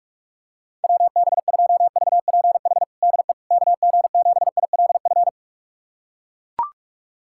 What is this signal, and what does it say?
Morse code, heard as a keyed tone, about 41 wpm, 700 Hz. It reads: WB2VPH DE KG7ILF
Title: USA Callsign DE Callsign 45wpm